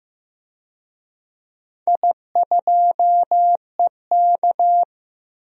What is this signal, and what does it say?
Morse code, heard as a keyed tone, about 15 wpm, 700 Hz. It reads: I2EK
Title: DXCC-100 Call Signs - 4 Characters - 15wpm